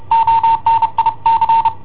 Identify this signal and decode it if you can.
Morse code, heard as a keyed tone, about 30 wpm, 910 Hz. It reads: ONIC
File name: i7ohp / cassetta i7ohp B